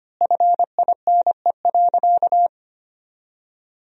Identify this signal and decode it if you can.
Morse code, heard as a keyed tone, about 25 wpm, 700 Hz. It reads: FINE.
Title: 4x Word-Spacing - Sentences from Top 500 Words - 25wpm